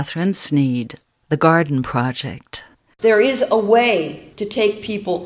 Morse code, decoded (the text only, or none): none